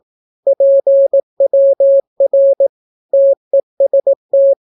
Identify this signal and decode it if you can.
Morse code, heard as a keyed tone, about 18 wpm, 550 Hz. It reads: PWR TEST
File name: naukapodstaw / skroty / skroty